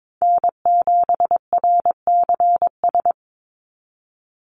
Morse code, heard as a keyed tone, about 22 wpm, 700 Hz. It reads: N7RCS
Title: Call Signs Repeated 3x - 22wpm